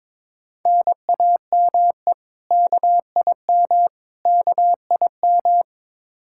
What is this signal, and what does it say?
Morse code, heard as a keyed tone, about 22 wpm, 700 Hz. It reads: NAME KIM KIM
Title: K1USN SST QSO Practice 22wpm